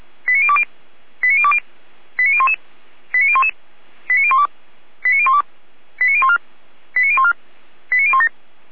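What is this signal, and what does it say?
Five-tone selective call (ZVEI), heard as quick runs of five tones, tones of about 70 ms. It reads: 89020 89020 89011 89011 89012 89012 89014 89014 89017
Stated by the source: Five-tone ZVEI selective call received by SDR, from audio.f85360.feuerwehr-orlamuende.org, a fire brigade (Feuerwehr Orlamünde).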